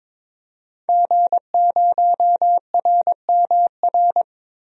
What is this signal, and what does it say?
Morse code, heard as a keyed tone, about 22 wpm, 700 Hz. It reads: G0RMR
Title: DXCC-100 Call Signs - 5 Characters - Repeated 3x - 22wpm